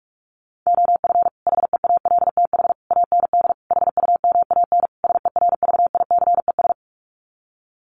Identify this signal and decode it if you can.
Morse code, heard as a keyed tone, about 44 wpm, 700 Hz. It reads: OF HEALTH AND HUMAN SERVICES
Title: Sets of 5 Words 45wpm